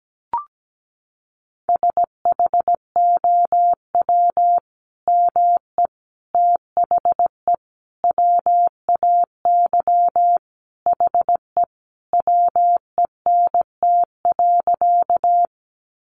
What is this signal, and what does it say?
Morse code, heard as a keyed tone, about 17 wpm, 700 Hz. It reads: SHOW ME THE WAY HE WENT.
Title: Sentences from Top 200 Words 17wpm